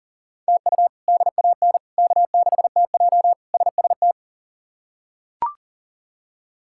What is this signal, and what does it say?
Morse code, heard as a keyed tone, about 40 wpm, 700 Hz. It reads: TU DAN K6TJ SST
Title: K1USN SST QSO Practice - 40wpm